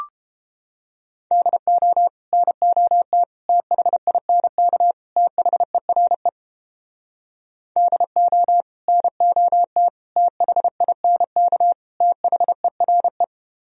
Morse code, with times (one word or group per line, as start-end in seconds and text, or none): 1.3-2.1 DO
2.3-3.2 NOT
3.5-4.9 THINK
5.2-6.3 THERE
7.8-8.6 DO
8.9-9.9 NOT
10.2-11.7 THINK
12.0-13.2 THERE